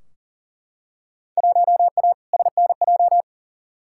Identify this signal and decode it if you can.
Morse code, heard as a keyed tone, about 40 wpm, 700 Hz. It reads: 1A SNJ
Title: Field Day Exchanges - 40wpm